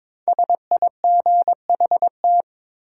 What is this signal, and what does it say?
Morse code, heard as a keyed tone, about 22 wpm, 700 Hz. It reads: SIGHT
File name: 5 Letter Words - First 3 Letters Given - 22wpm